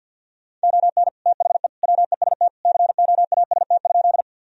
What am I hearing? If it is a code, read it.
Morse code, heard as a keyed tone, about 50 wpm, 700 Hz. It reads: ON THE WEST COAST?
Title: Sentences from Top 1000 Words - Review Part 4 - 50wpm